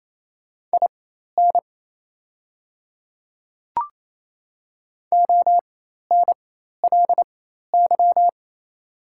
Morse code, heard as a keyed tone, about 28 wpm, spaced as for 13 wpm, 690 Hz. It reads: IN ONLY
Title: 4x Character Spacing - Top 200 Words - 28wpm